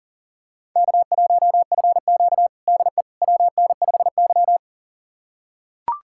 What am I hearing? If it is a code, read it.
Morse code, heard as a keyed tone, about 40 wpm, 700 Hz. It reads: K1FQ DE WN5Y
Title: USA Callsign DE Callsign - 4 Characters - 40wpm